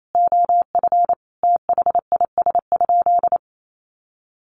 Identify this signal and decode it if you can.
Morse code, heard as a keyed tone, about 28 wpm, 700 Hz. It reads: OF THIS?
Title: Sentences from Top 200 Words 28wpm